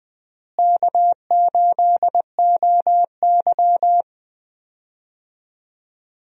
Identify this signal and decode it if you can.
Morse code, heard as a keyed tone, about 20 wpm, 700 Hz. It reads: K8OY
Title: USA Call Signs - 4 Characters - Repeated 3x - 20wpm